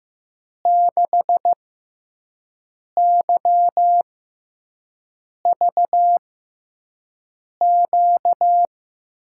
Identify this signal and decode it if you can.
Morse code, heard as a keyed tone, about 14 wpm, spaced as for 5 wpm, 700 Hz. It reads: 6YVQ
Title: USA Call Signs - 6 Characters - 6x Character Spacing - 15wpm